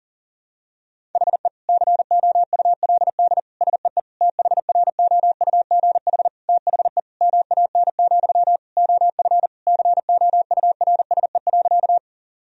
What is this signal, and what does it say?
Morse code, heard as a keyed tone, about 37 wpm, 700 Hz. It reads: HE COULD SEE THROUGH THE MAN, OF COURSE.
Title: Sentences from Top 1000 Words - Review Part 1 - 40wpm